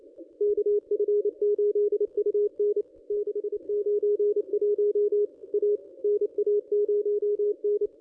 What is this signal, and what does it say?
Morse code, heard as a keyed tone, about 28 wpm, 410 Hz. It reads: KF8UN 691 A NA0N
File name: contests / ss-cw-k5tr-op-n5rz -1200